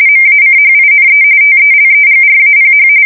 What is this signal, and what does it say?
RTTY, about 45.45 baud. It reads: MACHINE TELEGRAPH
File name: fsk170 / morkrum